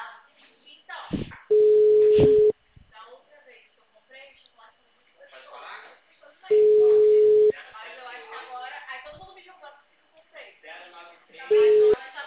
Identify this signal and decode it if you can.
Ringback tone (Europe, 425 Hz), with speech.